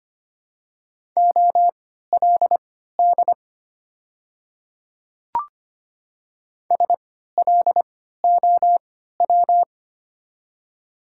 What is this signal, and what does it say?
Morse code, heard as a keyed tone, about 25 wpm, spaced as for 14 wpm, 700 Hz. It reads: OLD SLOW
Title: 3x Character Spacing - Top 500 Words - 25wpm